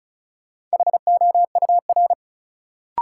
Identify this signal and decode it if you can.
Morse code, heard as a keyed tone, about 36 wpm, 700 Hz. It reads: HOUR
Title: Sentences from Top 400 Words - New Word Drill - 35wpm